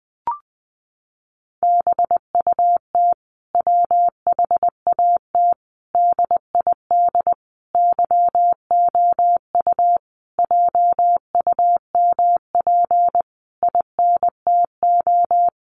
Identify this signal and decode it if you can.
Morse code, heard as a keyed tone, about 20 wpm, 700 Hz. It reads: BUT WHAT DID YOU JUMP INTO
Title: Sentences from Top 1300 Words 20wpm